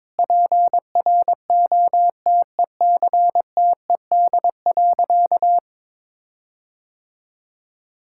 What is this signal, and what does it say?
Morse code, heard as a keyed tone, about 22 wpm, 700 Hz. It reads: PROTECTED.
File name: Sentences with 9-Word Phrases - 22wpm